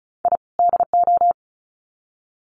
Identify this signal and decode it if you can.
Morse code, heard as a keyed tone, about 34 wpm, 700 Hz. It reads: I DO
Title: Sets of 2 Words 35wpm